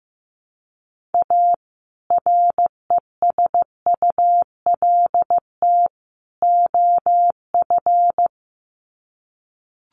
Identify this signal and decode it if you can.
Morse code, heard as a keyed tone, about 15 wpm, 700 Hz. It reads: A RESULT OF
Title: Sets of 3 Words 15wpm